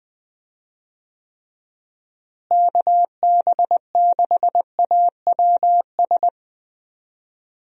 Morse code, heard as a keyed tone, about 20 wpm, 700 Hz. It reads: KB6AWS
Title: USA Call Signs - 6 Characters - Repeated 3x - 20wpm